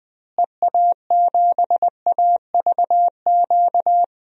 Morse code, heard as a keyed tone, about 21 wpm, 700 Hz. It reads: EA7AVQ
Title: DXCC-100 Call Signs - 6 Characters - Repeated 3x - 20wpm